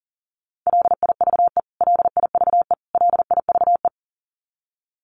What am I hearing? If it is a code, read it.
Morse code, heard as a keyed tone, about 40 wpm, 700 Hz. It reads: LIVE LIVE LIVE
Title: Top 300 Words Repeated 40wpm